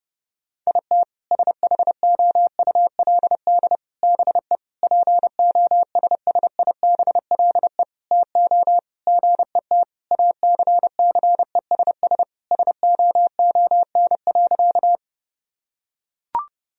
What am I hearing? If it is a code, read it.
Morse code, heard as a keyed tone, about 30 wpm, 700 Hz. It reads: IT SHOULD BE POSSIBLE TO GET ACCESS SOON.
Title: Sentences from Top 1000 Words - Review Part 4 - 30wpm